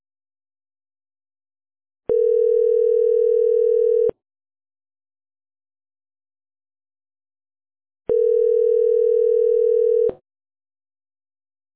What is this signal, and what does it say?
Ringback tone (North America), 440 + 480 Hz.